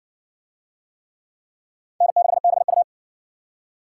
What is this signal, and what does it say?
Morse code, heard as a keyed tone, about 60 wpm, 700 Hz. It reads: N6BU